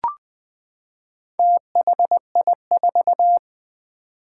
Morse code, heard as a keyed tone, about 20 wpm, 700 Hz. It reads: THI4
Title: Four Challenging Letter-Number Combinations 20wpm